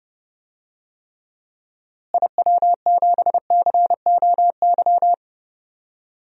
Morse code, heard as a keyed tone, about 31 wpm, 700 Hz. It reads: IW7COY
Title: DXCC-100 Call Signs - 6 Characters - Repeated 2x - 30wpm